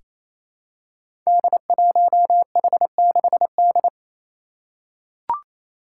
Morse code, heard as a keyed tone, about 28 wpm, 700 Hz. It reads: D1H6D
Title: Five Challenging Letter-Number Combinations 28wpm